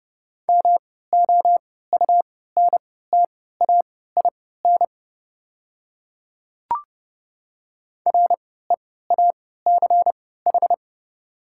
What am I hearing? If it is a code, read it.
Morse code, heard as a keyed tone, about 30 wpm, spaced as for 17 wpm, 690 Hz. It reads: MOUNTAIN REACH
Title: 3x Character Spacing - Top 500 Words - 30wpm